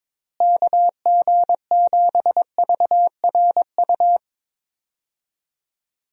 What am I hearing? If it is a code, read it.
Morse code, heard as a keyed tone, about 22 wpm, 700 Hz. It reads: KG7VRU
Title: USA Call Signs - 6 Characters - Repeated 3x - 22wpm